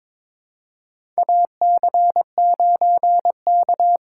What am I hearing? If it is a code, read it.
Morse code, heard as a keyed tone, about 21 wpm, 700 Hz. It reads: AC9K
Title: USA Call Signs - 3x Character Spacing - 22wpm